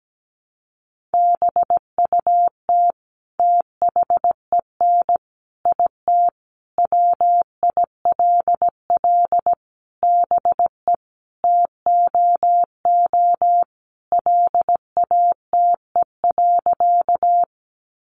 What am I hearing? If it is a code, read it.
Morse code, heard as a keyed tone, about 17 wpm, 700 Hz. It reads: BUT THEN IT WILL BE TOO LATE.
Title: Sentences from Top 300 Words 17wpm